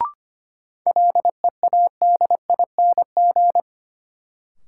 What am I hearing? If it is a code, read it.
Morse code, heard as a keyed tone, about 25 wpm, 700 Hz. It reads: LEADING
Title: Sentences from Top 1100 Words - New Word Drill - 25wpm